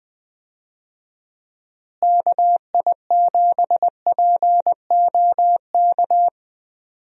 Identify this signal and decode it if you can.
Morse code, heard as a keyed tone, about 20 wpm, 700 Hz. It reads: KI7POK
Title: USA Call Signs - 6 Characters - Repeated 3x - 20wpm